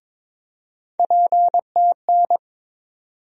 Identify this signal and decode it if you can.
Morse code, heard as a keyed tone, about 22 wpm, 700 Hz. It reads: PTN
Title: USA Call Sign Postfixes - 3 Letters 22wpm